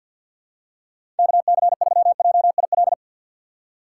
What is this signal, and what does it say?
Morse code, heard as a keyed tone, about 47 wpm, 700 Hz. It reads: KC3JIL
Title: USA Call Signs - 6 Characters - 3x Character Spacing - 50wpm